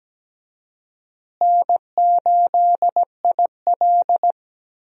Morse code, heard as a keyed tone, about 17 wpm, 700 Hz. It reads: N8IL
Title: USA Call Signs - 4 Characters - Repeated 2x - 17wpm